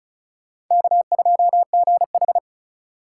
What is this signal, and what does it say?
Morse code, heard as a keyed tone, about 43 wpm, 700 Hz. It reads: K2GH
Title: Speed Racing - USA 1x2 Call Signs 35wpm